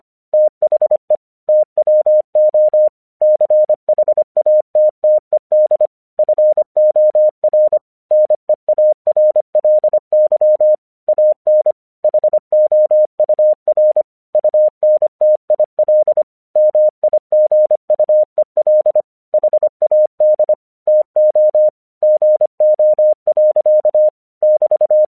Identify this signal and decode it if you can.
Morse code, heard as a keyed tone, about 25 wpm, 610 Hz. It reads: THE TWO CHATTED FOR NEARLY AN HOUR UNTIL MIGUEL HAD TO GO. <BT>
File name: 1 English 25wpm / AQC31